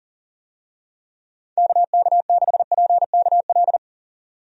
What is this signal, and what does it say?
Morse code, heard as a keyed tone, about 40 wpm, 700 Hz. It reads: KK6PKL
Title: USA Call Signs - 6 Characters - 40wpm